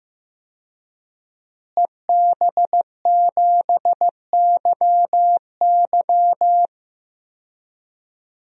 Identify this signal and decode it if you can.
Morse code, heard as a keyed tone, about 15 wpm, 700 Hz. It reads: EB7YY
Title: DXCC-100 Call Signs - 5 Characters - Repeated 3x - 15wpm